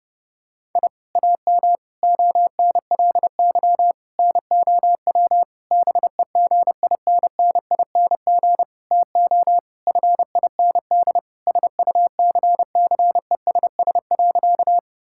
Morse code, heard as a keyed tone, about 30 wpm, 700 Hz. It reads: I AM ONLY NOW BEGINNING TO FIND SUCCESS.
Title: Sentences from Top 1000 Words - Review Part 5 - 30wpm